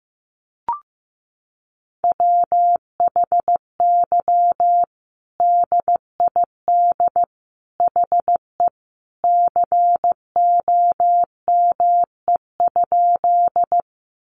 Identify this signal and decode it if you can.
Morse code, heard as a keyed tone, about 14 wpm, 700 Hz. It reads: WHY DID HE COME?
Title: Sentences from Top 200 Words 15wpm